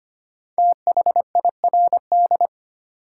Morse code, heard as a keyed tone, about 25 wpm, 700 Hz. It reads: THIRD
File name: 5 Letter Words - First 3 Letters Given - 25wpm